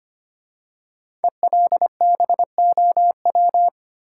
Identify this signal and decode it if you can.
Morse code, heard as a keyed tone, about 25 wpm, 700 Hz. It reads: ELBOW
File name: 5 Letter Words - First 3 Letters Given - 25wpm